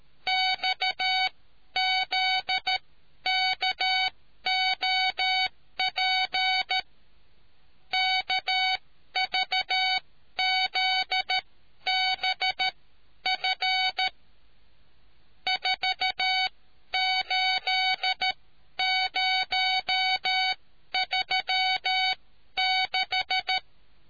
Morse code, heard as a keyed tone, about 13 wpm, spaced as for 11 wpm, 750 Hz. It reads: XZKOP KVZBF 48036